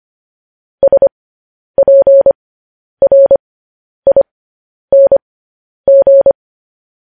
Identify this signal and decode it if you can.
Morse code, heard as a keyed tone, about 25 wpm, spaced as for 10 wpm, 550 Hz. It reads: SPRING
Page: news-headlines-in-morse-code-at-10-wpm